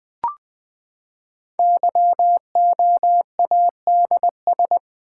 Morse code, heard as a keyed tone, about 20 wpm, 700 Hz. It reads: YOADS
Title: Code Groups - 5 Letter:Numbers - 20wpm